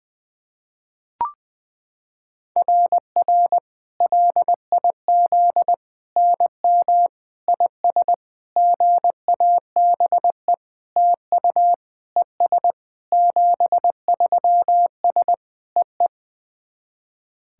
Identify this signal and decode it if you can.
Morse code, heard as a keyed tone, about 20 wpm, 700 Hz. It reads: RR LIZ NM IS GABE TU ES 73S EE
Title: SOTA QSO Practice 20wpm